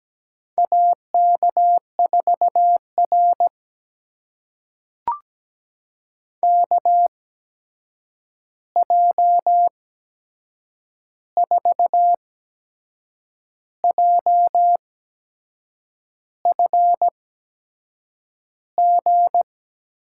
Morse code, heard as a keyed tone, about 17 wpm, spaced as for 6 wpm, 700 Hz. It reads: AK4R KJ4JFG
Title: USA Call Signs - 8x Character Spacing - 17wpm